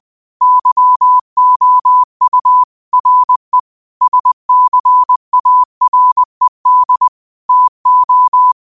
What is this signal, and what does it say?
Morse code, heard as a keyed tone, about 20 wpm, 1000 Hz. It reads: YOU RE SCARED TO
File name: qotdfiles / 20wpm